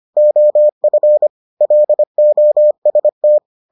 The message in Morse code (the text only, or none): OF LOST